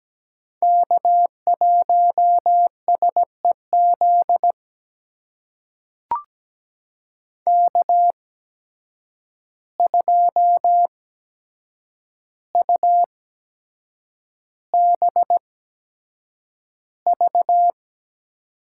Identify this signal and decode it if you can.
Morse code, heard as a keyed tone, about 17 wpm, spaced as for 7 wpm, 700 Hz. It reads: K1SEZ K2UBV
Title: USA Call Signs - 5 Characters - 8x Character Spacing - 17wpm